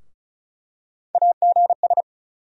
Morse code, heard as a keyed tone, about 36 wpm, 700 Hz. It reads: AGS
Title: USA Call Sign Postfixes - 3 Letters 35wpm